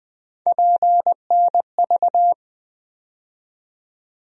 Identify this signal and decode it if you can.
Morse code, heard as a keyed tone, about 20 wpm, 700 Hz. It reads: PNV
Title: USA Call Sign Postfixes - 3 Letters 20wpm